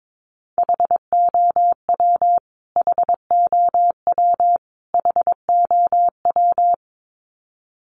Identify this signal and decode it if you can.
Morse code, heard as a keyed tone, about 22 wpm, 700 Hz. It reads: HOW HOW HOW
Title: Top 500 Words Repeated 22wpm